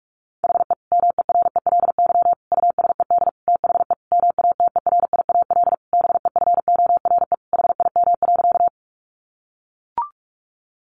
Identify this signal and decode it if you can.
Morse code, heard as a keyed tone, about 44 wpm, 700 Hz. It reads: HE MERELY USED THE MATERIAL BEFORE HIM.
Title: Sentences from Top 1400 Words 45wpm